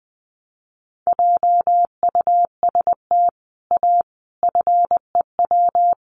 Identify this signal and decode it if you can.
Morse code, heard as a keyed tone, about 20 wpm, 700 Hz. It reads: JUST A FEW
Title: Sets of 5 Words Encore 20wpm